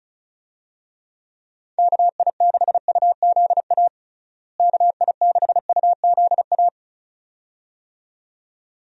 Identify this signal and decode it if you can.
Morse code, heard as a keyed tone, about 36 wpm, 700 Hz. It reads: KI6UZA KI6UZA
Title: USA Call Signs - 6 Characters - Repeated 2x - 35wpm